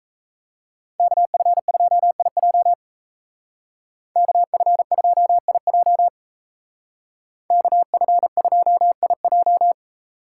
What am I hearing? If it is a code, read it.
Morse code, heard as a keyed tone, about 32 wpm, 700 Hz. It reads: KF2IJ KF2IJ KF2IJ
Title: Speed Racing - USA 2x2 Call Signs 28wpm